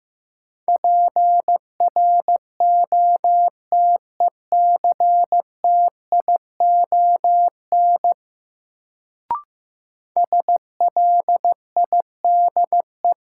Morse code, heard as a keyed tone, about 16 wpm, 700 Hz. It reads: PROTECTION SLIDE
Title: Sentences from Top 1600 Words - New Word Drill - 15wpm